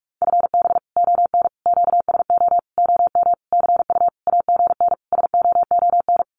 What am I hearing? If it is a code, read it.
Morse code, heard as a keyed tone, about 44 wpm, 700 Hz. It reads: FB ON QSO OM CU AGN SOON
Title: Common QSO Phrases 45wpm